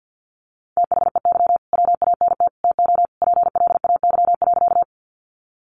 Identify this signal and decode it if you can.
Morse code, heard as a keyed tone, about 50 wpm, 700 Hz. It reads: THEY WANT TO PLAY.
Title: Easy Phrases 50wpm